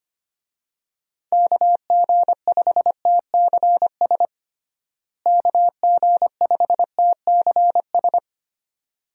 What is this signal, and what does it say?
Morse code, heard as a keyed tone, about 25 wpm, 700 Hz. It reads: KG5TCS KG5TCS
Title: USA Call Signs - 6 Characters - Repeated 2x - 25wpm